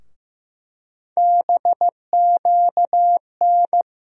Morse code, heard as a keyed tone, about 15 wpm, 700 Hz. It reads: BQN
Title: USA Call Sign Postfixes - 3 Letters 15wpm